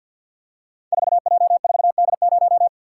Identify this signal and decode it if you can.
Morse code, heard as a keyed tone, about 52 wpm, 700 Hz. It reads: 4J4D0
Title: Code Groups - 5 Letter:Numbers - 50wpm